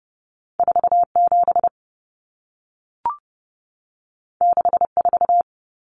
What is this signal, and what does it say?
Morse code, heard as a keyed tone, about 30 wpm, 700 Hz. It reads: 47 64
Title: Numbers - 2-Digit 30wpm